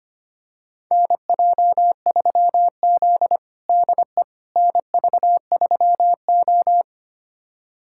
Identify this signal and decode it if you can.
Morse code, heard as a keyed tone, about 25 wpm, 700 Hz. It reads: NJ3Z DE NV3O
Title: USA Callsign DE Callsign - 4 Characters - 25wpm